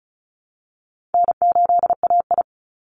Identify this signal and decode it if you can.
Morse code, heard as a keyed tone, about 34 wpm, 700 Hz. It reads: N8AI
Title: Easy Call Signs Repeated 3x - 35wpm